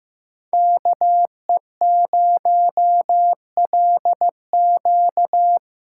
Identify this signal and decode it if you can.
Morse code, heard as a keyed tone, about 15 wpm, 700 Hz. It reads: KE0LQ
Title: USA Call Signs - 5 Characters - 2x Character Spacing - 15wpm